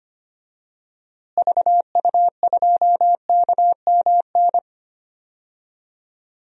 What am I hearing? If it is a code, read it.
Morse code, heard as a keyed tone, about 25 wpm, 700 Hz. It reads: VU2KMN
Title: DXCC-100 Call Signs - 6 Characters - Repeated 2x - 25wpm